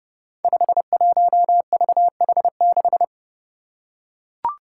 Morse code, heard as a keyed tone, about 30 wpm, 700 Hz. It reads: 51VH6
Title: Five Challenging Letter-Number Combinations 30wpm